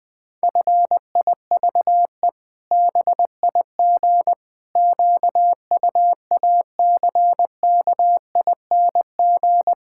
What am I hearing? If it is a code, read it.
Morse code, heard as a keyed tone, about 20 wpm, 700 Hz. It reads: FIVE BIG QUACKING
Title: Pangram Challenge v2 - 20wpm